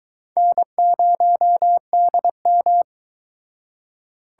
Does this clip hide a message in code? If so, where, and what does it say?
Yes: Morse code from 0.4 to 2.8 s, heard as a keyed tone, about 22 wpm, 700 Hz. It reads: N0DM